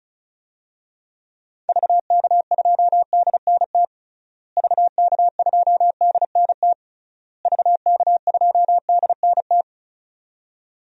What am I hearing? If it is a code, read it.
Morse code, heard as a keyed tone, about 34 wpm, 700 Hz. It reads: VK2DNT VK2DNT VK2DNT
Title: DXCC-100 Call Signs - 6 Characters - Repeated 3x - 35wpm